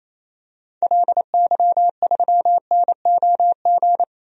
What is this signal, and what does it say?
Morse code, heard as a keyed tone, about 28 wpm, 700 Hz. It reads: LY3NOG
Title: DXCC-100 Call Signs - 6 Characters - 28wpm